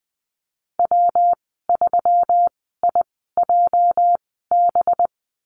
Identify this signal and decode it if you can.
Morse code, heard as a keyed tone, about 20 wpm, spaced as for 15 wpm, 700 Hz. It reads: W3IJB
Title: 231 Lesson 37g - Letter x - Call Signs 20wpm